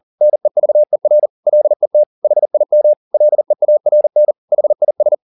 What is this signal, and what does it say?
Morse code, heard as a keyed tone, about 40 wpm, 600 Hz. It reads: NEVER LET HIM LEARN HIS